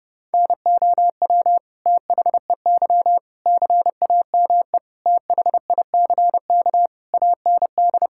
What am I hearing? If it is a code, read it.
Morse code, heard as a keyed tone, about 29 wpm, 700 Hz. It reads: NOW THEY CAME THICK AND
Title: Sentences from Top 1600 Words 30wpm